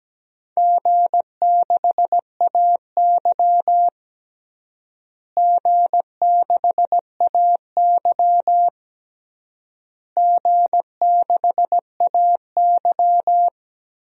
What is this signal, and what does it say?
Morse code, heard as a keyed tone, about 17 wpm, 700 Hz. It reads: G6AY G6AY G6AY
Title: DXCC-100 Call Signs - 4 Characters - Repeated 3x - 17wpm